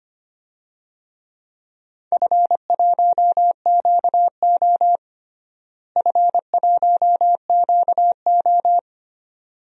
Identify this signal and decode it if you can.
Morse code, heard as a keyed tone, about 25 wpm, 700 Hz. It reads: F1QO F1QO
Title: DXCC-100 Call Signs - 4 Characters - Repeated 2x - 25wpm